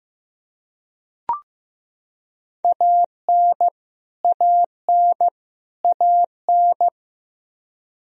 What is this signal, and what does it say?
Morse code, heard as a keyed tone, about 15 wpm, 700 Hz. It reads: AN AN AN
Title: Top 100 Words Repeated 15wpm